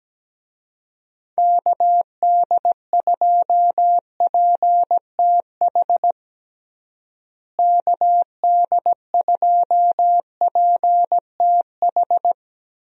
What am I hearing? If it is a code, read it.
Morse code, heard as a keyed tone, about 17 wpm, 700 Hz. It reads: KD2PTH KD2PTH
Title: USA Call Signs - 6 Characters - Repeated 3x - 17wpm